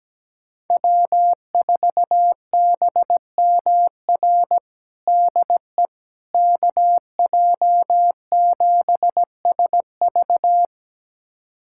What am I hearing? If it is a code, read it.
Morse code, heard as a keyed tone, about 17 wpm, 700 Hz. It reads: W4BMR DE KJ7SV
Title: USA Callsign DE Callsign - 5 Characters - 17wpm